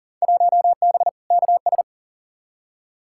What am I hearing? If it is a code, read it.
Morse code, heard as a keyed tone, about 40 wpm, 700 Hz. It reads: 1B KS
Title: Field Day Exchanges Encore - 40wpm